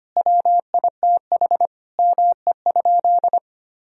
Morse code, heard as a keyed tone, about 25 wpm, 700 Hz. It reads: WITH ME?
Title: Sentences from Top 1000 Words - Review Part 2 - 25wpm